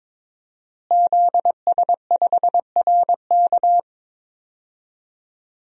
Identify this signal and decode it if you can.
Morse code, heard as a keyed tone, about 22 wpm, 700 Hz. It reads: ZS5RK
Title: DXCC-100 Call Signs - 5 Characters - Repeated 2x - 22wpm